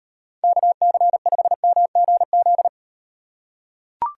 Morse code, heard as a keyed tone, about 34 wpm, 700 Hz. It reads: KC5MGZ